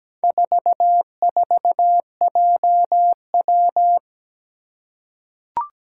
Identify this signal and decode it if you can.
Morse code, heard as a keyed tone, about 17 wpm, 700 Hz. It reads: 44JW
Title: Four Challenging Letter-Number Combinations 17wpm